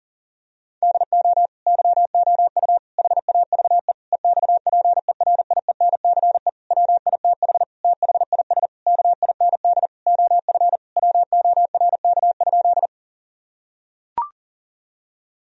Morse code, heard as a keyed tone, about 37 wpm, 700 Hz. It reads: DO YOU HAVE EXPERIENCE WITH THIS KIND OF WORK?
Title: Sentences from Top 1000 Words - Review Part 3 - 40wpm